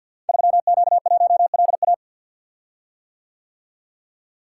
Morse code, heard as a keyed tone, about 52 wpm, 700 Hz. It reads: 3Q1LA
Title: Code Groups - 5 Letter:Numbers - 50wpm